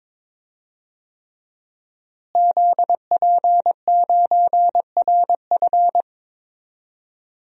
Morse code, heard as a keyed tone, about 21 wpm, 700 Hz. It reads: ZP9RF
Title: DXCC-100 Call Signs - 5 Characters - Repeated 3x - 22wpm